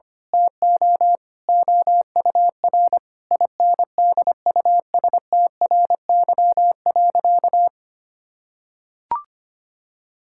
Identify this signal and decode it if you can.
Morse code, heard as a keyed tone, about 25 wpm, 700 Hz. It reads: TO OUR INDUSTRY.
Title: Sentences from Top 1000 Words - Review Part 6 - 25wpm